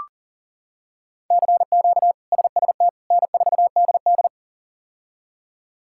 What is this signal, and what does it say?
Morse code, heard as a keyed tone, about 40 wpm, 700 Hz. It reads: CQ SST N4DD
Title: K1USN SST QSO Practice - 40wpm